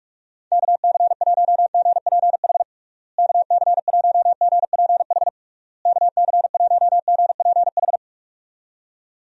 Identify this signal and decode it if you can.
Morse code, heard as a keyed tone, about 46 wpm, 700 Hz. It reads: KC1GPH KC1GPH KC1GPH
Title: USA Call Signs - 6 Characters - Repeated 3x - 45wpm